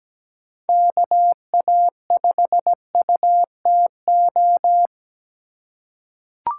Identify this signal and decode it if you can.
Morse code, heard as a keyed tone, about 17 wpm, 700 Hz. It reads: KA5UTO
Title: USA Call Signs - 8x Character Spacing - 17wpm